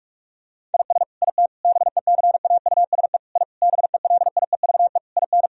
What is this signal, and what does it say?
Morse code, heard as a keyed tone, about 42 wpm, 700 Hz. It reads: IS IT BECAUSE I BELIEVE IN
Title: Sentences from Top 1000 Words - Review Part 3 - 45wpm